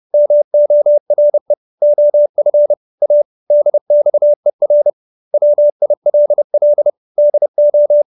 Morse code, heard as a keyed tone, about 30 wpm, 590 Hz. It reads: MORE OF A DXER WILL DO
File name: QSO315 30